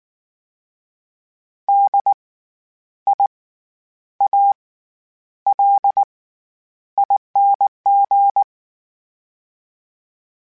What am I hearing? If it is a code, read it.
Morse code, heard as a keyed tone, about 19 wpm, spaced as for 9 wpm, 800 Hz. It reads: DIALING
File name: Special INGs / Special INGs-mp3s / SlowToFaster ing @18